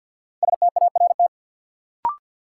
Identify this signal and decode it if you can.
Morse code, heard as a keyed tone, about 53 wpm, 700 Hz. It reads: START
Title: Sentences from Top 300 Words - New Word Drill - 50wpm